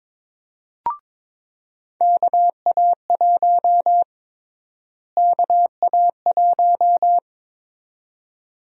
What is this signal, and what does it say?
Morse code, heard as a keyed tone, about 22 wpm, 700 Hz. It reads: KA1 KA1
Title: USA Call Sign Prefixes - Repeated 2x - 22wpm